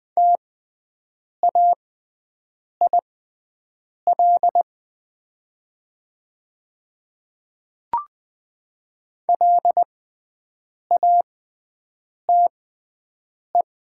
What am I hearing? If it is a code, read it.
Morse code, heard as a keyed tone, about 20 wpm, spaced as for 7 wpm, 700 Hz. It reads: TAIL LATE
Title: 6x Character Spacing - Top 500 Words - 20wpm